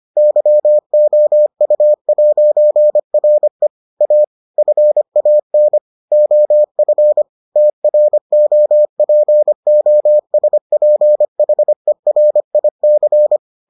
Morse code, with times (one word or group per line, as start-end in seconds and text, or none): 0.2-3.7 YOU'RE
4.0-4.2 A
4.6-5.8 FAN
6.1-7.2 OF
7.6-13.4 TROPOSPHERIC